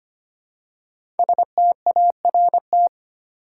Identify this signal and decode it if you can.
Morse code, heard as a keyed tone, about 25 wpm, 700 Hz. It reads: START
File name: 5 Letter Words - First 3 Letters Given - 25wpm